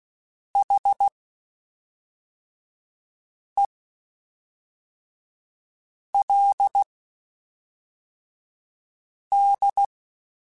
Morse code, heard as a keyed tone, about 16 wpm, spaced as for 3 wpm, 800 Hz. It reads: HELD